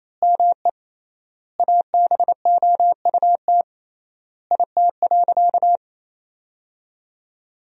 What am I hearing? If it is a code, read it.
Morse code, heard as a keyed tone, about 28 wpm, 700 Hz. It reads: ME ABOUT IT.
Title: 3x Word-Spacing - Sentences from Top 200 Words - 28wpm